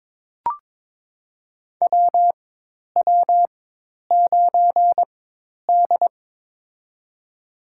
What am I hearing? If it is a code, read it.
Morse code, heard as a keyed tone, about 23 wpm, spaced as for 10 wpm, 700 Hz. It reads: WW9D
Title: USA Call Signs - 4 Characters - 4x Character Spacing - 22wpm